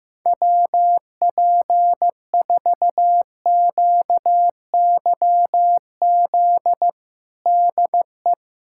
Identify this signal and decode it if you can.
Morse code, heard as a keyed tone, about 15 wpm, 700 Hz. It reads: WP4QYZ DE
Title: USA Callsign DE Callsign - 6 Characters - 15wpm